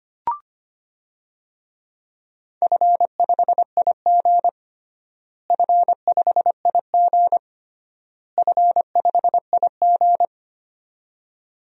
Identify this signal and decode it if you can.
Morse code, heard as a keyed tone, about 25 wpm, 700 Hz. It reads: F5IG F5IG F5IG
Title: DXCC-100 Call Signs - 4 Characters - Repeated 3x - 25wpm